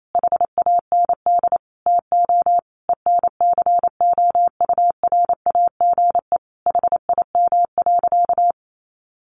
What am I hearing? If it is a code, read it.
Morse code, heard as a keyed tone, about 28 wpm, 700 Hz. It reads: HAND TO ENCOURAGE HIM.
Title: Sentences from Top 1400 Words 28wpm